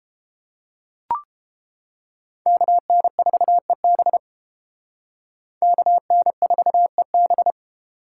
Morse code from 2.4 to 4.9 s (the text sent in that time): KN4EB